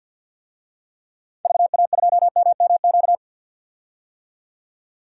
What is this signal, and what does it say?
Morse code, heard as a keyed tone, about 45 wpm, 700 Hz. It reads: VA2MMQ
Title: DXCC-100 Call Signs - 6 Characters - 50wpm